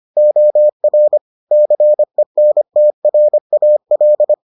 Morse code, heard as a keyed tone, about 25 wpm, 600 Hz. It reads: OR CENTRAL